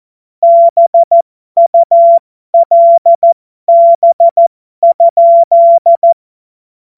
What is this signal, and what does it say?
Morse code, heard as a keyed tone, about 14 wpm, spaced as for 12 wpm, 680 Hz. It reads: BULB?
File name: Morsekurs / Witze